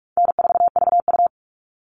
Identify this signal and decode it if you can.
Morse code, heard as a keyed tone, about 44 wpm, 700 Hz. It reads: N4VU
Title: All Cut Numbers - 4-Digit 45wpm